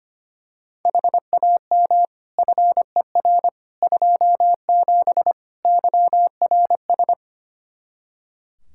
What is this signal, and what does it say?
Morse code, heard as a keyed tone, about 25 wpm, 700 Hz. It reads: HAM FER 27 YRS
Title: Practice QSO - Exchange 3 - More Info - 25wpm